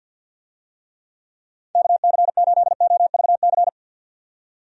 Morse code, heard as a keyed tone, about 45 wpm, 700 Hz. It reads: KC8OVC
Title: USA Call Signs - 6 Characters - 50wpm